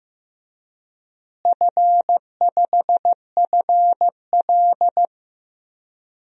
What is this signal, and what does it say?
Morse code, heard as a keyed tone, about 16 wpm, 700 Hz. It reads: F5FL
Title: DXCC-100 Call Signs - 4 Characters - Repeated 3x - 15wpm